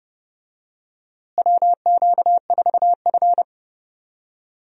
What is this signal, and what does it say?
Morse code, heard as a keyed tone, about 32 wpm, 700 Hz. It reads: WQ4F
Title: USA Call Signs - 4 Characters - Repeated 2x - 30wpm